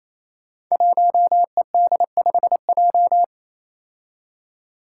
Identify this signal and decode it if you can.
Morse code, heard as a keyed tone, about 28 wpm, 700 Hz. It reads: 1ED5J
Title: Five Challenging Letter-Number Combinations 28wpm